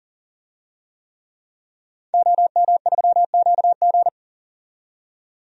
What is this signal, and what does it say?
Morse code, heard as a keyed tone, about 37 wpm, 700 Hz. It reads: OM3QG
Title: DXCC-100 Call Signs - 5 Characters - 40wpm